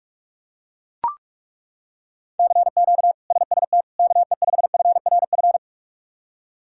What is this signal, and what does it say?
Morse code, heard as a keyed tone, about 41 wpm, 700 Hz. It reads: CQ SST KE5FRF
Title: K1USN SST QSO Practice - 45wpm